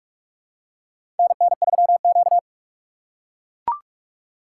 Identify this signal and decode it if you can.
Morse code, heard as a keyed tone, about 42 wpm, 700 Hz. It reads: NN3Q
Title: USA Call Signs - 8x Character Spacing - 45wpm